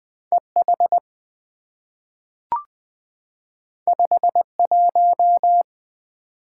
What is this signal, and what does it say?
Morse code, heard as a keyed tone, about 20 wpm, 700 Hz. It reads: EH 51
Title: Two Letter-Number Combinations - 20wpm